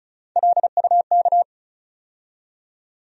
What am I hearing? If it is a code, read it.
Morse code, heard as a keyed tone, about 36 wpm, 700 Hz. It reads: LUK
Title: USA Call Sign Postfixes - 3 Letters 35wpm